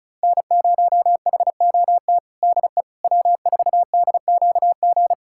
Morse code, heard as a keyed tone, about 36 wpm, 700 Hz. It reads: N0HOT DE W4DQG
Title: USA Callsign DE Callsign - 5 Characters - 35wpm